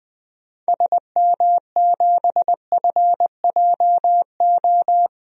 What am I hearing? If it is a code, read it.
Morse code, heard as a keyed tone, about 21 wpm, 700 Hz. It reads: SM7FJO
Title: DXCC-100 Call Signs - 6 Characters - Repeated 2x - 20wpm